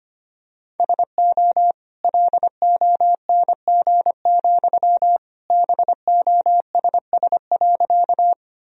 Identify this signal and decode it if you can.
Morse code, heard as a keyed tone, about 25 wpm, 700 Hz. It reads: SO LONG, BOSS.
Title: Sentences from Top 1000 Words - Review Part 4 - 25wpm